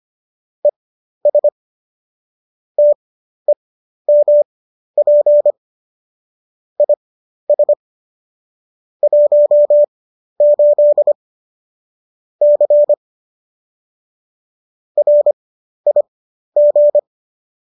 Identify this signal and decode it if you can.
Morse code, heard as a keyed tone, about 25 wpm, spaced as for 12 wpm, 590 Hz. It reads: ES TEMP IS 18 C RIG
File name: qso009 12